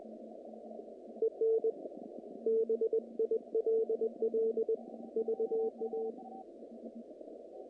Morse code, heard as a keyed tone, about 20 wpm, 450 Hz. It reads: EN BILL VA
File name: contests / naqp-cw-k5tr-op-k5pi- -2300